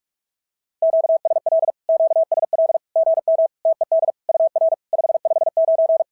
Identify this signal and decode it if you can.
Morse code, heard as a keyed tone, about 44 wpm, 660 Hz. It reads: QSL QSL GM TED UR 559
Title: SOTA QSO Practice 45wpm